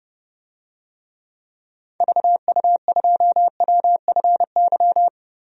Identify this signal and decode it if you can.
Morse code, heard as a keyed tone, about 30 wpm, 700 Hz. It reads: VU2WFY
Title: DXCC-100 Call Signs - 6 Characters - 30wpm